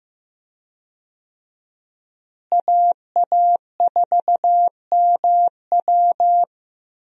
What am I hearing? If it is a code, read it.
Morse code, heard as a keyed tone, about 16 wpm, 700 Hz. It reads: AA4MW
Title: USA Call Signs - 5 Characters - Repeated 2x - 15wpm